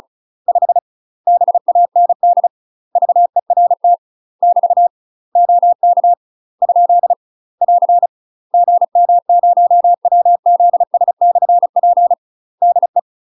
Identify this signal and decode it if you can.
Morse code, heard as a keyed tone, about 35 wpm, 700 Hz. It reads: 5 BAND VERT <BT> OK ? <AR> GM0WZS/P DE
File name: UK MASTER QSOs TRAINING / 35WPM / UK MASTER QSOs 0004